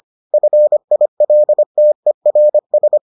Morse code, heard as a keyed tone, about 25 wpm, 600 Hz. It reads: FILTERS